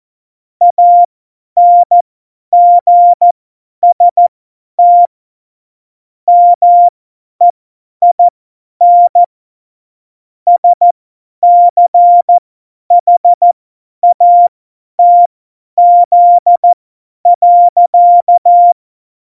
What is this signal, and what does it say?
Morse code, heard as a keyed tone, about 14 wpm, spaced as for 10 wpm, 700 Hz. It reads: ANGST MEIN SCHATZ.
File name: Morsekurs / Witze / Witz 03.Juli 2018, 50 F70